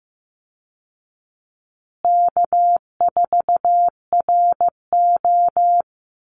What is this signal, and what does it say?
Morse code, heard as a keyed tone, about 15 wpm, 700 Hz. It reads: K4RO
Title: Easy Call Signs Repeated 3x - 15wpm